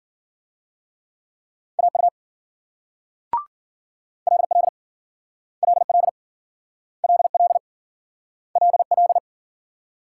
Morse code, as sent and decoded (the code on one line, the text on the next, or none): .- ..- / .-.. .-.. / .-.. .-.. / .-.. .-.. / .-.. .-..
AU LL LL LL LL